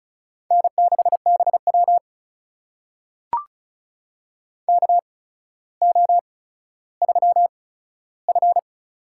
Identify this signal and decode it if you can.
Morse code, heard as a keyed tone, about 36 wpm, spaced as for 15 wpm, 700 Hz. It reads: N6BW KO3F
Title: USA Call Signs - 4 Characters - 8x Character Spacing - 35wpm